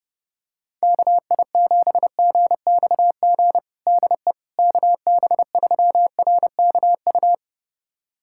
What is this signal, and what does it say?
Morse code, heard as a keyed tone, about 32 wpm, 700 Hz. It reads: KI7GXG DE KB3RKU
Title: USA Callsign DE Callsign 30wpm